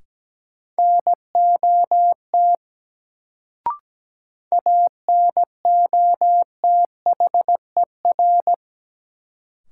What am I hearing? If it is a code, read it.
Morse code, heard as a keyed tone, about 17 wpm, 700 Hz. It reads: NOT ANOTHER
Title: Sentences from Top 600 Words - New Word Drill - 17wpm